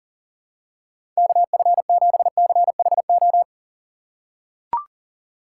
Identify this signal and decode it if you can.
Morse code, heard as a keyed tone, about 40 wpm, 700 Hz. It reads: KF7CHO
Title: USA Call Signs - 6x Character Spacing - 40wpm